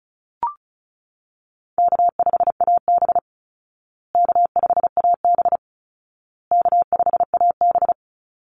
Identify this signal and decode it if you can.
Morse code, heard as a keyed tone, about 34 wpm, 700 Hz. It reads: K5AB K5AB K5AB
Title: Easy Call Signs Repeated 3x - 35wpm